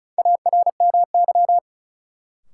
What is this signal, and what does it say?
Morse code, heard as a keyed tone, about 34 wpm, 700 Hz. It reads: ARMY
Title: Sentences from Top 800 Words - New Word Drill - 35wpm